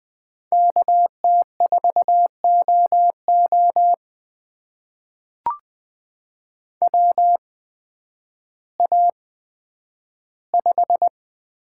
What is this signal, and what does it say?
Morse code, heard as a keyed tone, about 20 wpm, spaced as for 11 wpm, 700 Hz. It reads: KT4OO WA5
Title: USA Call Signs - 8x Character Spacing - 20wpm